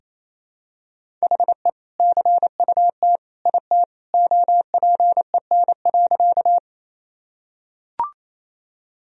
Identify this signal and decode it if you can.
Morse code, heard as a keyed tone, about 28 wpm, 700 Hz. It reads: HE CUT IT OPEN.
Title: Sentences from Top 1000 Words - Review Part 1 - 28wpm